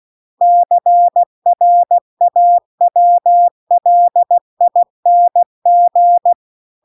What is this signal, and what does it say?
Morse code, heard as a keyed tone, about 16 wpm, 700 Hz. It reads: CRAWLING